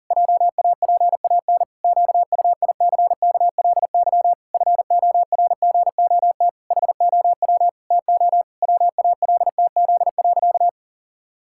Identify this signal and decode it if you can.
Morse code, heard as a keyed tone, about 40 wpm, 700 Hz. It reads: JAPAN QUICKLY FORGOT HOW TO WALTZ.
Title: Pangram Challenge v2 - 40wpm